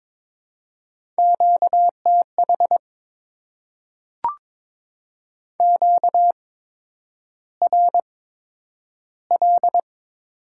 Morse code, heard as a keyed tone, about 22 wpm, spaced as for 9 wpm, 700 Hz. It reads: QTH QRL
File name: Top 20 QSO Elements - 8x Character Spacing - 22wpm